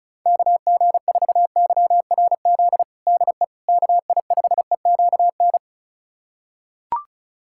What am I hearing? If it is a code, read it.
Morse code, heard as a keyed tone, about 34 wpm, 690 Hz. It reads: KG4YRZ DE KI5EQN
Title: USA Callsign DE Callsign - 6 Characters - 35wpm